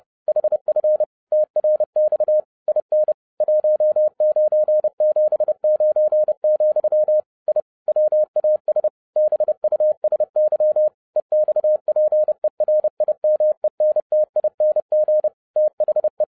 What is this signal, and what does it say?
Morse code, heard as a keyed tone, about 30 wpm, 610 Hz. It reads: HF TRX IN 1979, I WAS BUSY EXPERIMENTING THE
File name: 2012-01-mp3 / n11-I DIDNT MISS THE SOLAR CYCLE 24-30wpm